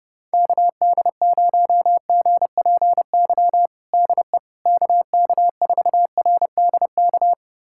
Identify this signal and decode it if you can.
Morse code, heard as a keyed tone, about 30 wpm, 700 Hz. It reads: KD0GPY DE KK4RDK
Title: USA Callsign DE Callsign 30wpm